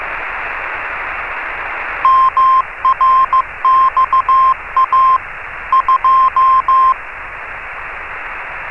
Morse code, heard as a keyed tone, about 15 wpm, 1050 Hz, with radio noise. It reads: MRXA 2
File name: RS30-CW-Beacon